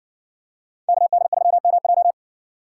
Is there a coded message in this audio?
Yes: Morse code from 0.9 to 2.1 s, heard as a keyed tone, about 60 wpm, 700 Hz. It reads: KD3MJ